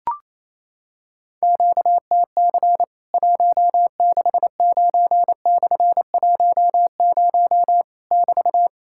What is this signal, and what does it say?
Morse code, heard as a keyed tone, about 28 wpm, 700 Hz. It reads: QTC 169/10 <BT>
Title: Worked All Europe DX Contest - QTC Exchange - 28wpm